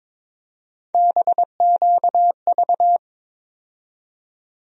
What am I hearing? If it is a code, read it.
Morse code, heard as a keyed tone, about 22 wpm, 700 Hz. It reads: BQV
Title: USA Call Sign Postfixes - 3 Letters 22wpm